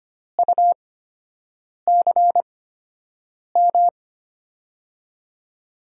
Morse code, heard as a keyed tone, about 25 wpm, spaced as for 7 wpm, 700 Hz. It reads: UCM
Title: USA Call Signs - 6 Characters - 8x Character Spacing - 25wpm